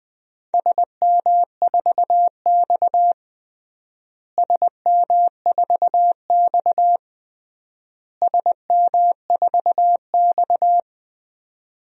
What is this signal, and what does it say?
Morse code, heard as a keyed tone, about 20 wpm, 700 Hz. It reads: SM4X SM4X SM4X
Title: DXCC-100 Call Signs - 4 Characters - Repeated 3x - 20wpm